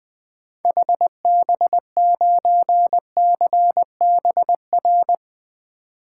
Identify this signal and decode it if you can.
Morse code, heard as a keyed tone, about 20 wpm, 700 Hz. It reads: HB9CBR
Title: 2023 SOTA Activator Call Signs - 20wpm